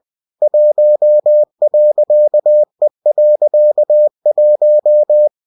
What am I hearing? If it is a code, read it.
Morse code, heard as a keyed tone, about 20 wpm, 600 Hz. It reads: 1.E.1